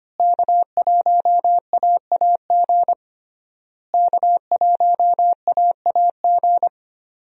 Morse code, heard as a keyed tone, about 25 wpm, 700 Hz. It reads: K1AAG K1AAG
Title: USA Call Signs - 5 Characters - Repeated 3x - 25wpm